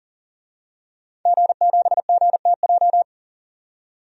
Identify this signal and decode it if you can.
Morse code, heard as a keyed tone, about 40 wpm, 700 Hz. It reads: G7GTJ
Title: DXCC-100 Call Signs - 5 Characters - 40wpm